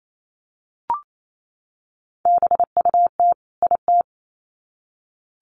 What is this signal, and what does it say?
Morse code, heard as a keyed tone, about 28 wpm, 700 Hz. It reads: BUT IT
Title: Sets of 2 Words 28wpm